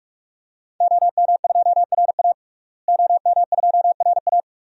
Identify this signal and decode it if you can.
Morse code, heard as a keyed tone, about 46 wpm, 700 Hz. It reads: OM2RA OM2RA
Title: DXCC-100 Call Signs - 5 Characters - Repeated 2x - 45wpm